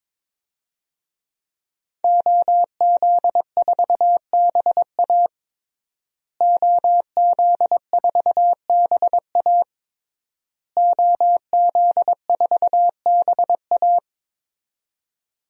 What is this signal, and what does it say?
Morse code, heard as a keyed tone, about 22 wpm, 700 Hz. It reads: OZ4BA OZ4BA OZ4BA
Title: DXCC-100 Call Signs - 5 Characters - Repeated 3x - 22wpm